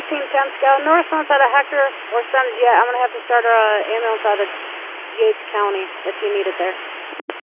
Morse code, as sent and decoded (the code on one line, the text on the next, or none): none